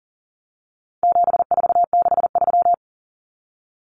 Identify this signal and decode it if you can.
Morse code, heard as a keyed tone, about 40 wpm, 700 Hz. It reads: 7463
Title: Common Cut Numbers - 4-Digit 40wpm